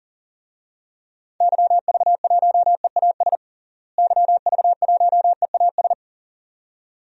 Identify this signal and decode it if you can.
Morse code, heard as a keyed tone, about 40 wpm, 700 Hz. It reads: YV1EAS YV1EAS
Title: DXCC-100 Call Signs - 6 Characters - Repeated 2x - 40wpm